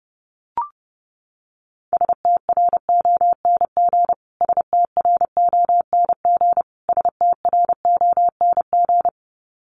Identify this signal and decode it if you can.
Morse code, heard as a keyed tone, about 30 wpm, 700 Hz. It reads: STRONG STRONG STRONG
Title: Top 500 Words Repeated 30wpm